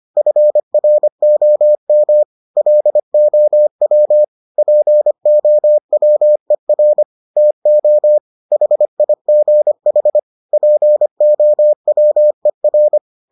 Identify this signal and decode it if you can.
Morse code, heard as a keyed tone, about 25 wpm, 600 Hz. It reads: FROM LOW POWER TO HIGH POWER